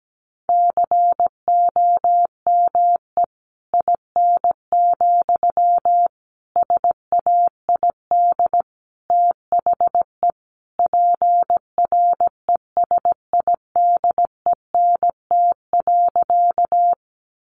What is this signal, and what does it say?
Morse code, heard as a keyed tone, about 17 wpm, 700 Hz. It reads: COME IN, SAID THE PRESIDENT.
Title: Sentences from Top 1000 Words 17wpm